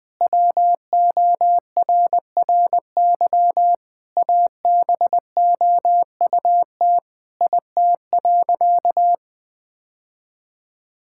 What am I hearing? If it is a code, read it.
Morse code, heard as a keyed tone, about 20 wpm, 700 Hz. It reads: WORRY ABOUT IT.
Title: Sentences with 6-Word Phrases - Spoken Beginning - 20wpm